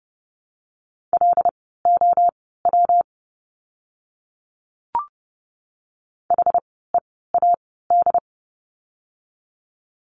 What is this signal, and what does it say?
Morse code, heard as a keyed tone, about 30 wpm, spaced as for 17 wpm, 700 Hz. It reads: LOW HEAD
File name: course / 30:10wpm / 115 Lesson 19f - Letter w - Words 30wpm